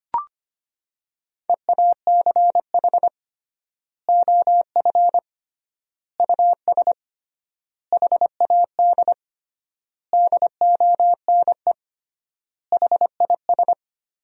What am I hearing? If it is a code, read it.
Morse code, heard as a keyed tone, about 25 wpm, 700 Hz. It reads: EACH OF US HAD DONE HIS
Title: 3x Word-Spacing - Sentences from Top 500 Words - 25wpm